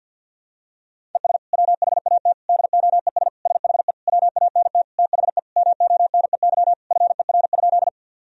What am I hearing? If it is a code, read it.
Morse code, heard as a keyed tone, about 51 wpm, 700 Hz. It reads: ES WHAT DOES SHE WANT THE MONEY FER?
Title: Sentences from Top 400 Words and Abbreviations 50wpm